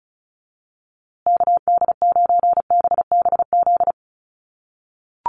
Morse code, heard as a keyed tone, about 34 wpm, 700 Hz. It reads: KD9BBZ
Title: USA Call Signs Repeated 3x 35wpm